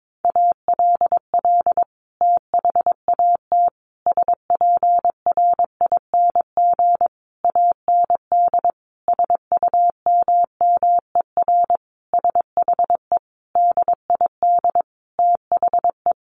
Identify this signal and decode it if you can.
Morse code, heard as a keyed tone, about 22 wpm, 700 Hz. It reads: ALL THAT SPRING AND SUMMER SHE DID THE
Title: Sentences from Top 1200 Words 22wpm